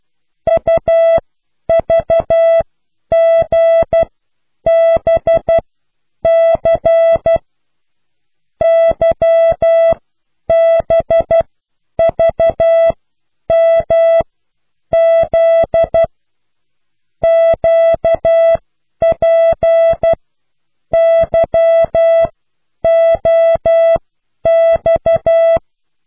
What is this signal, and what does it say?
Morse code, heard as a keyed tone, about 12 wpm, spaced as for 9 wpm, 660 Hz. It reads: UVGBC YBVMZ QPYOX